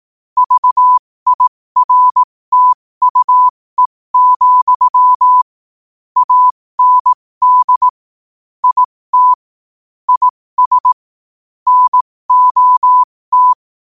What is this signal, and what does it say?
Morse code, heard as a keyed tone, about 18 wpm, spaced as for 15 wpm, 1000 Hz. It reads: VIRTUE, AND IT IS NOT
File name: qotdfiles / 15wpm